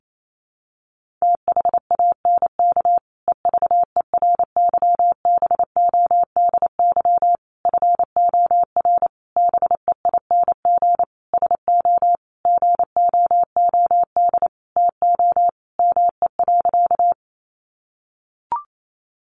Morse code, heard as a keyed tone, about 28 wpm, 700 Hz. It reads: THANK EVERYBODY FOR BEING SO GOOD TO ME.
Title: Sentences from Top 1400 Words 28wpm